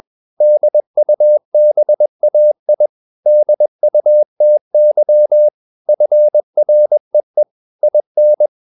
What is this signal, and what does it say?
Morse code, heard as a keyed tone, about 21 wpm, 600 Hz. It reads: DUBAI DUTY FREE IN
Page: CW med Gnister